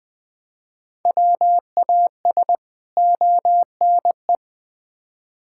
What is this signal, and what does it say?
Morse code, heard as a keyed tone, about 21 wpm, 700 Hz. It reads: WAS ONE
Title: Build A Phrase - 5-Word Phrases - 20wpm